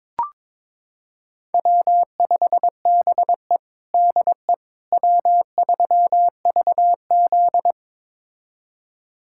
Morse code, heard as a keyed tone, about 22 wpm, 700 Hz. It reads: W5BE DE W3VZ
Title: USA Callsign DE Callsign - 4 Characters - 22wpm